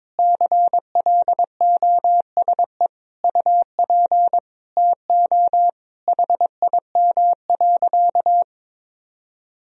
Morse code, heard as a keyed tone, about 22 wpm, 700 Hz. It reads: CLOSE UP TO HIM.
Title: Sentences from Top 1000 Words - Review Part 1 - 22wpm